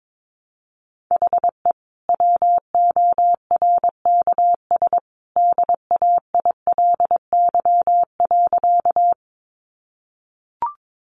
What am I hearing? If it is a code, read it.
Morse code, heard as a keyed tone, about 22 wpm, 700 Hz. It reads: HE WORKS DAILY.
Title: Sentences from Top 1100 Words 22wpm